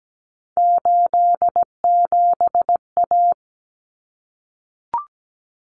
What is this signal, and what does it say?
Morse code, heard as a keyed tone, about 17 wpm, 700 Hz. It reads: 87A
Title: Common Cut Numbers - 3-Digit 17wpm